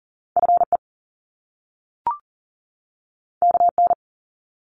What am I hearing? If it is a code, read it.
Morse code, heard as a keyed tone, about 41 wpm, 700 Hz. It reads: FE KN
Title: Top Two Letter Combinations in Words 40wpm